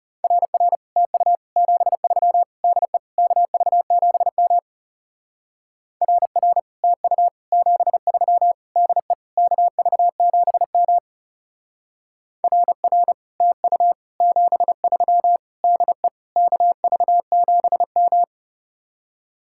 Morse code, heard as a keyed tone, about 37 wpm, 700 Hz. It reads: RR TU 73 DE KV7M RR TU 73 DE KV7M RR TU 73 DE KV7M
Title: Speed-Racing Common QSO Phrases Encore - 30wpm